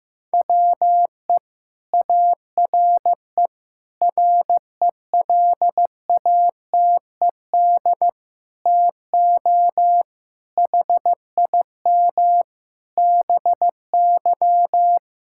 Morse code, heard as a keyed tone, about 15 wpm, 700 Hz. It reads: WE ARE RELATED TO HIM BY
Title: Sentences from Top 1000 Words - Review Part 5 - 15wpm